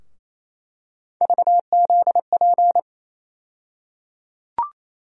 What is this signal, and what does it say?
Morse code, heard as a keyed tone, about 28 wpm, 700 Hz. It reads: VZP
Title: USA Call Sign Postfixes - 3 Letters 28wpm